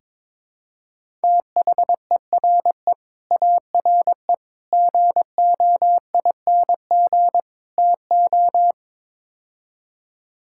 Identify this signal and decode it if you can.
Morse code, heard as a keyed tone, about 22 wpm, 700 Hz. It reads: THERE ARE GOING TO
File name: practice / 22wpm / Build A Phrase - 5-Word Phrases - 22wpm